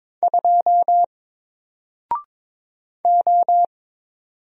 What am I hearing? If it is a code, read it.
Morse code, heard as a keyed tone, about 22 wpm, 700 Hz. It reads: 2 O
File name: Single-Letter-Number-22wpm